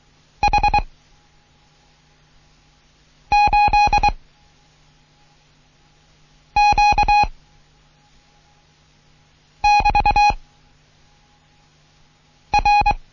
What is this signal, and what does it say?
Morse code, heard as a keyed tone, about 23 wpm, 830 Hz. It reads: H 8 Q <BT> R
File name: Morsen / Kurs A 16